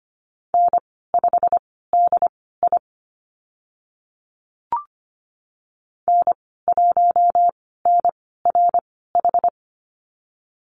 Morse code, heard as a keyed tone, about 25 wpm, spaced as for 16 wpm, 700 Hz. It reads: N5DI N1NRH
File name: Lesson 15g - Number 5 - Call Signs 25wpm